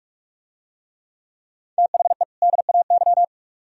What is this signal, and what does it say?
Morse code, heard as a keyed tone, about 44 wpm, 700 Hz. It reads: THE DAY